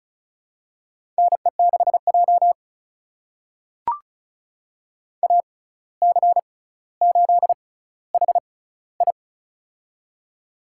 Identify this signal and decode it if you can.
Morse code, heard as a keyed tone, about 36 wpm, spaced as for 17 wpm, 700 Hz. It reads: NE6J AC8HI
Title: USA Call Signs - 6x Character Spacing - 35wpm